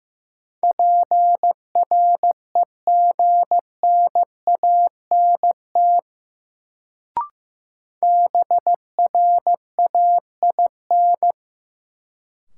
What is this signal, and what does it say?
Morse code, heard as a keyed tone, about 15 wpm, 700 Hz. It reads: PREGNANT BRAIN
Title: Sentences from Top 1000 Words - Review Part 6 - New Word Drill - 15wpm